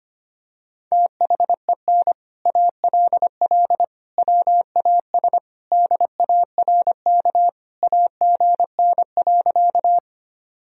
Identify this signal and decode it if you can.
Morse code, heard as a keyed tone, about 25 wpm, 700 Hz. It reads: THEN ALL WAS DARK AGN.
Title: Sentences from Top 500 Words and Abbreviations 25wpm